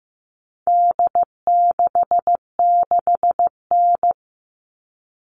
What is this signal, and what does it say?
Morse code, heard as a keyed tone, about 15 wpm, 700 Hz. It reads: D66N
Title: All Cut Numbers - 4-Digit 15wpm